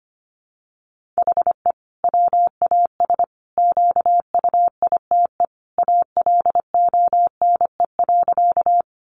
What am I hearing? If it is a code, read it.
Morse code, heard as a keyed tone, about 25 wpm, 700 Hz. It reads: HE WAS QUITE ALONE.
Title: Sentences from Top 1000 Words 25wpm